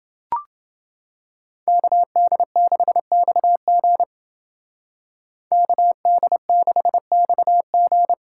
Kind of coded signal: Morse code

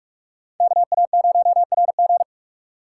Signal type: Morse code

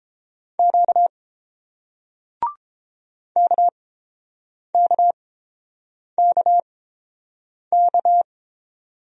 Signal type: Morse code

